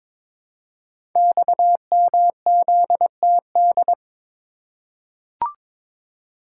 Morse code, heard as a keyed tone, about 23 wpm, 700 Hz. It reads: XMZTD
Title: Code Groups - 5 Letter:Numbers - 22wpm